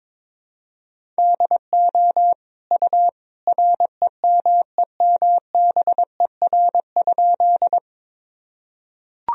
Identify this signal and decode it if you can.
Morse code, heard as a keyed tone, about 22 wpm, 700 Hz. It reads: DO U REMEMBER?
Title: Sentences from Top 400 Words and Abbreviations 22wpm